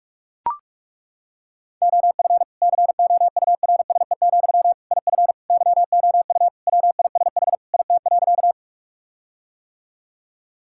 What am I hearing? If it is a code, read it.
Morse code, heard as a keyed tone, about 41 wpm, 700 Hz. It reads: OF COURSE, IF YOU WISH IT.
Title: Sentences from Top 1600 Words 45wpm